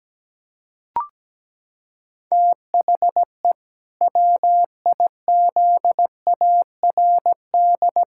Morse code, heard as a keyed tone, about 17 wpm, 700 Hz. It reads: THE WIZARD
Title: Pangram Challenge v2 - 17wpm